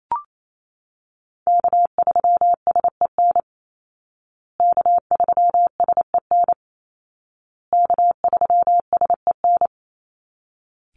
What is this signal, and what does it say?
Morse code, heard as a keyed tone, about 28 wpm, 700 Hz. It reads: K3SEN K3SEN K3SEN
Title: Call Signs Repeated 2x - 28wpm